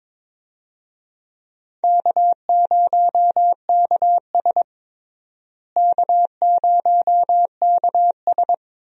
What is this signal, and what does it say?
Morse code, heard as a keyed tone, about 21 wpm, 700 Hz. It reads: K0KS K0KS
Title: USA Call Signs - 4 Characters - Repeated 2x - 22wpm